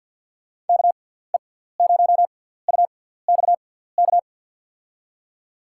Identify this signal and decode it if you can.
Morse code, heard as a keyed tone, about 50 wpm, spaced as for 17 wpm, 700 Hz. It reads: KE0UXK
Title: USA Call Signs - 6 Characters - 6x Character Spacing - 50wpm